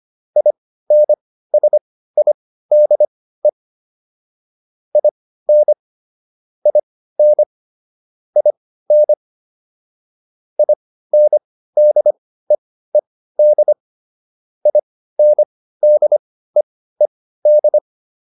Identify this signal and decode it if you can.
Morse code, heard as a keyed tone, about 25 wpm, spaced as for 15 wpm, 600 Hz. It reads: INSIDE IN IN IN INDEED INDEED